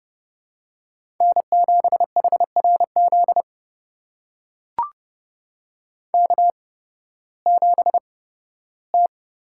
Morse code, heard as a keyed tone, about 32 wpm, spaced as for 16 wpm, 700 Hz. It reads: N7HRZ K7T
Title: USA Call Signs - 5 Characters - 8x Character Spacing - 30wpm